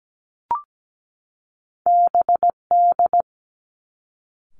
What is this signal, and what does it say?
Morse code, heard as a keyed tone, about 17 wpm, 700 Hz. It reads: BD
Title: Two Challenging Letter-Number Combinations 17wpm